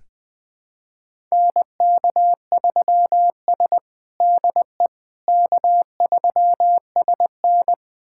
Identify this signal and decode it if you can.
Morse code, heard as a keyed tone, about 20 wpm, 700 Hz. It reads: NK3S DE K3SN
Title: USA Callsign DE Callsign - 4 Characters - 20wpm